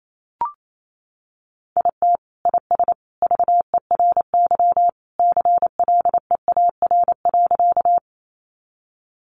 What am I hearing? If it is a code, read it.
Morse code, heard as a keyed tone, about 28 wpm, 700 Hz. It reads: IT IS VERY CLEAR.
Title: Sentences from Top 500 Words 28wpm